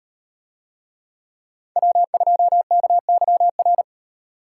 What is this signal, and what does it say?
Morse code, heard as a keyed tone, about 38 wpm, 700 Hz. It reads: W2KYR